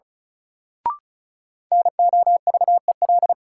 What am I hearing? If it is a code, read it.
Morse code, heard as a keyed tone, about 36 wpm, 700 Hz. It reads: NOVEL
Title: Sentences from Top 1300 Words - New Word Drill - 35wpm